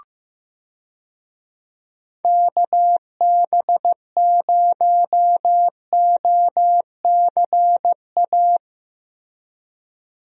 Morse code, heard as a keyed tone, about 15 wpm, 700 Hz. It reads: KB0OCA
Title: USA Call Signs - 6 Characters - Repeated 2x - 15wpm